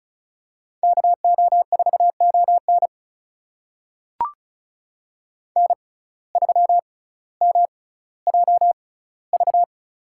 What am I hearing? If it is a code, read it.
Morse code, heard as a keyed tone, about 36 wpm, spaced as for 18 wpm, 700 Hz. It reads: KO4ON N3MJV
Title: USA Call Signs - 5 Characters - 6x Character Spacing - 35wpm